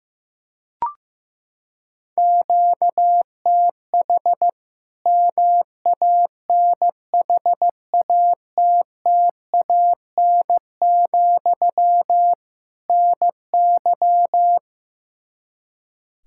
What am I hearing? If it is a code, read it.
Morse code, heard as a keyed tone, about 16 wpm, 700 Hz. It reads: QTH MANHATTAN, NY
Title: Practice QSO - Exchange 1 - Beginning - 15wpm